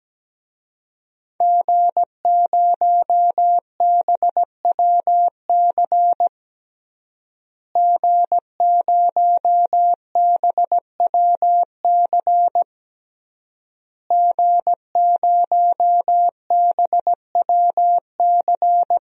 Morse code, heard as a keyed tone, about 17 wpm, 700 Hz. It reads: G0BWC G0BWC G0BWC
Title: DXCC-100 Call Signs - 5 Characters - Repeated 3x - 17wpm